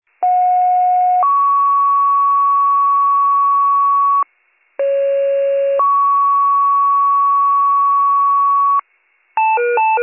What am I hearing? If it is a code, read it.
Two-tone page, heard as a one-second tone, then a three-second tone, about 730 then 1120 Hz, about 550 then 1120 Hz. It is heard from 0.2 to 8.8 s.